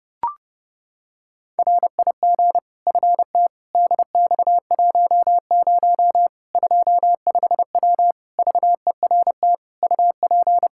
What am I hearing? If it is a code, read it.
Morse code, heard as a keyed tone, about 32 wpm, 700 Hz. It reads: RIG FT DX10 25W VERT UP
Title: K1USN SST QSO Practice 30wpm